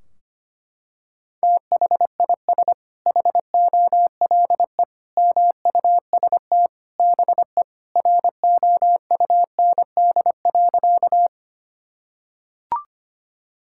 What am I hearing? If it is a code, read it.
Morse code, heard as a keyed tone, about 25 wpm, 700 Hz. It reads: THIS HOLE MUST BE ROUND.
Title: Sentences from Top 1600 Words 25wpm